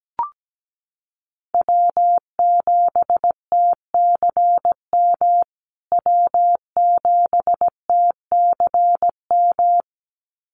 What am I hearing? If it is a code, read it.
Morse code, heard as a keyed tone, about 17 wpm, 700 Hz. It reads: W7TCM W7TCM
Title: USA Call Signs Repeated 2x 17wpm